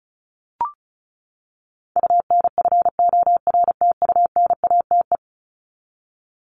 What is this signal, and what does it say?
Morse code, heard as a keyed tone, about 34 wpm, 700 Hz. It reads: UNFORTUNATE
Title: LY Suffix - Top 100 Words with Base Words 35wpm